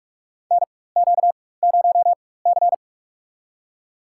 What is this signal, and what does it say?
Morse code, heard as a keyed tone, about 42 wpm, spaced as for 20 wpm, 700 Hz. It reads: NQ0C
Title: USA Call Signs - 4 Characters - 4x Character Spacing - 45wpm